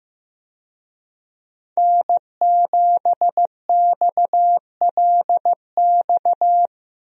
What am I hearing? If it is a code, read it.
Morse code, heard as a keyed tone, about 15 wpm, 700 Hz. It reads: N7XLX
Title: USA Call Signs - 5 Characters - 15wpm